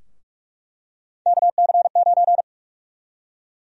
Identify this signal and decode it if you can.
Morse code, heard as a keyed tone, about 46 wpm, 700 Hz. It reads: KC9
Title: USA Call Sign Prefixes - 45wpm